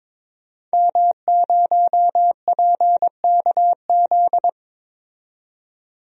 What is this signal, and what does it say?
Morse code, heard as a keyed tone, about 22 wpm, 700 Hz. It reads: M0PKZ
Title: DXCC-100 Call Signs - 5 Characters - 22wpm